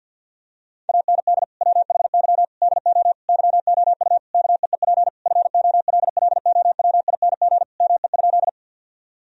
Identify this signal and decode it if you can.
Morse code, heard as a keyed tone, about 52 wpm, 700 Hz. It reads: AND WHY DO YOU KEEP FOLLOWING ME?
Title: Sentences from Top 1000 Words - Review Part 4 - 50wpm